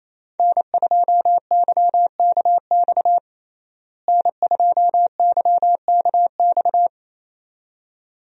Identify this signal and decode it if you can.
Morse code, heard as a keyed tone, about 28 wpm, 700 Hz. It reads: N2YKX N2YKX
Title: USA Call Signs - 5 Characters - Repeated 2x - 28wpm